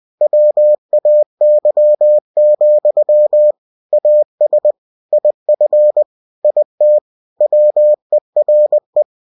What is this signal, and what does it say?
Morse code, heard as a keyed tone, about 20 wpm, 600 Hz. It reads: WAY, AS IF IT WERE
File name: cw-books / winnie-the-pooh / winnie-20wpm-page-0037